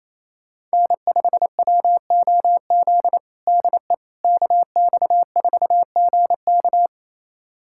Morse code, heard as a keyed tone, about 28 wpm, 700 Hz. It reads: N5WOZ DE KX4GK
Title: USA Callsign DE Callsign - 5 Characters - 28wpm